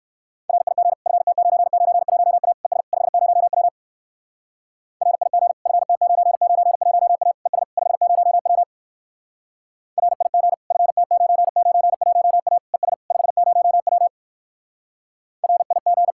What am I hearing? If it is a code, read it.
Morse code, heard as a keyed tone, about 59 wpm, 700 Hz. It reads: RIG FT991A ES 50W RIG FT991A ES 50W RIG FT991A ES 50W RIG